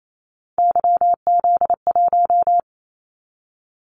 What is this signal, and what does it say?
Morse code, heard as a keyed tone, about 28 wpm, 700 Hz. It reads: YZ1
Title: Callsign Prefixes for Top 25 DXCC - 28wpm